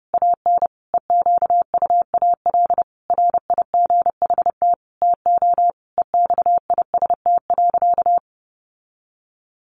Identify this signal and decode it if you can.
Morse code, heard as a keyed tone, about 30 wpm, 700 Hz. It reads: AN EQUAL RIGHT TO EXIST.
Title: Sentences from Top 1400 Words 30wpm